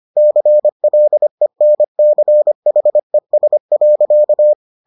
Morse code, heard as a keyed tone, about 25 wpm, 600 Hz. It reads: CLENCHES.